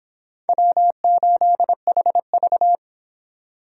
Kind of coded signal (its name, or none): Morse code